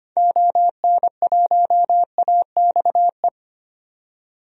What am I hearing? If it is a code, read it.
Morse code, heard as a keyed tone, about 25 wpm, 700 Hz. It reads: ON1AXE
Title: DXCC-100 Call Signs - 6 Characters - 25wpm